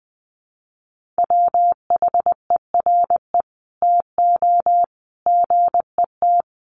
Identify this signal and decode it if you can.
Morse code, heard as a keyed tone, about 20 wpm, 700 Hz. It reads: WHERE TO GET
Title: Sentences from Top 1400 Words 20wpm